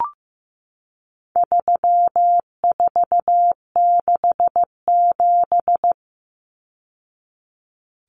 Morse code, heard as a keyed tone, about 15 wpm, 700 Hz. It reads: 3467
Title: Numbers - 4-Digit 15wpm